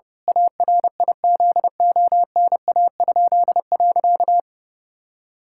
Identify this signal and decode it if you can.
Morse code, heard as a keyed tone, about 30 wpm, 700 Hz. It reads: ARIZONA?.
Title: Pangram Challenge v2 - 30wpm